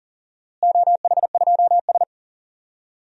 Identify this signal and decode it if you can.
Morse code, heard as a keyed tone, about 40 wpm, 700 Hz. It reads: OH2S
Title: DXCC-100 Call Signs - 4 Characters - Repeated 2x - 40wpm